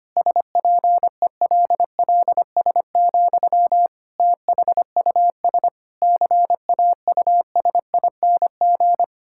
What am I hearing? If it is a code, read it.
Morse code, heard as a keyed tone, about 25 wpm, 700 Hz. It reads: SPELLS, THUS CAUSING
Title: Pangram Challenge v2 - 25wpm